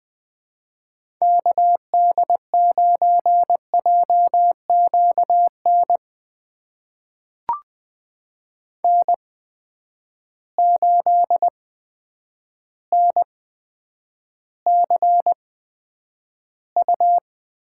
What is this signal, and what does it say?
Morse code, heard as a keyed tone, about 20 wpm, spaced as for 9 wpm, 700 Hz. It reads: KD9JQN N8NCU
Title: USA Call Signs - 8x Character Spacing - 20wpm